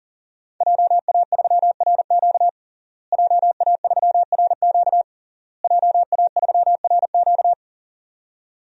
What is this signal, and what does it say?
Morse code, heard as a keyed tone, about 40 wpm, 700 Hz. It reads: JA3RQ JA3RQ JA3RQ
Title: DXCC-100 Call Signs - 5 Characters - Repeated 3x - 40wpm